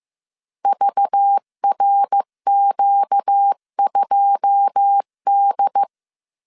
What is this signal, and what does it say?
Morse code, heard as a keyed tone, about 15 wpm, 800 Hz. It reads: VRQ2D
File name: CW / DET / 1 9 1500C S B 14wpm 3302